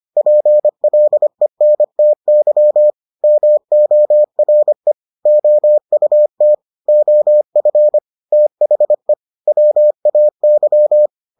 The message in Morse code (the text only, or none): PLENTY MORE OUT OF THE WAY